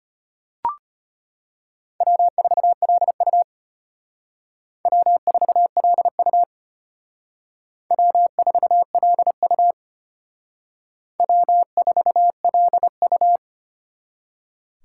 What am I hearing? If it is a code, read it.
Morse code, heard as a keyed tone, about 32 wpm, 700 Hz. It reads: W4LU W4LU W4LU W4LU